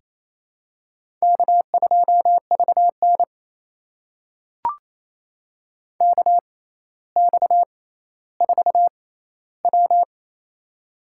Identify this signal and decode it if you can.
Morse code, heard as a keyed tone, about 28 wpm, spaced as for 14 wpm, 700 Hz. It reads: K2VN KX4W
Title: USA Call Signs - 4 Characters - 6x Character Spacing - 28wpm